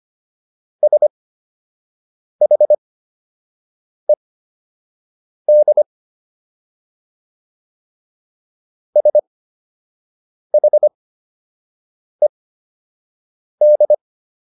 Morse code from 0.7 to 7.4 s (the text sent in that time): SHED